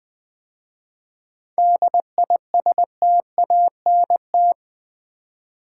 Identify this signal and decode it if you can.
Morse code, heard as a keyed tone, about 20 wpm, 700 Hz. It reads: DISTANT
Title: 7 Letter Words - First 3 Letters Given - 20wpm